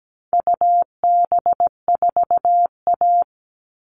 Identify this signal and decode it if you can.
Morse code, heard as a keyed tone, about 17 wpm, 700 Hz. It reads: UB4A
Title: All Cut Numbers - 4-Digit 17wpm